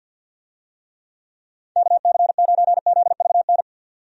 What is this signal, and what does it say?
Morse code, heard as a keyed tone, about 50 wpm, 700 Hz. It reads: KC9ZVN
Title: USA Call Signs - 6 Characters - 50wpm